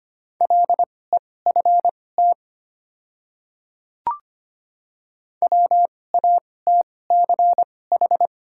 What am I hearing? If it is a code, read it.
Morse code, heard as a keyed tone, about 25 wpm, spaced as for 18 wpm, 700 Hz. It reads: LEFT WATCH
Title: 2x Character Spacing - Top 300 Words - 25wpm